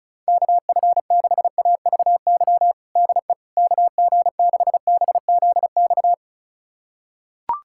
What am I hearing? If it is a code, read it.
Morse code, heard as a keyed tone, about 36 wpm, 700 Hz. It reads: KF6AVY DE KG6BZX
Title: USA Callsign DE Callsign - 6 Characters - 35wpm